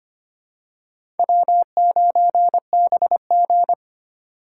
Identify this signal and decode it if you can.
Morse code, heard as a keyed tone, about 25 wpm, 700 Hz. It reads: W9BG
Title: USA Call Signs - 4 Characters - 6x Character Spacing - 25wpm